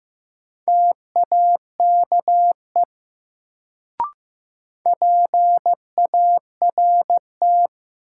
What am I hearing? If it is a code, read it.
Morse code, heard as a keyed tone, about 15 wpm, 700 Hz. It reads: TAKE PART
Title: Sentences from Top 200 Words - New Word Drill - 15wpm